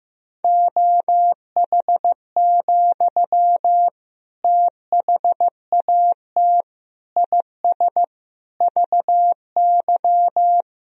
Morse code, heard as a keyed tone, about 15 wpm, 700 Hz. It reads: OH, THAT IS VY
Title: Sentences from Top 500 Words and Abbreviations 15wpm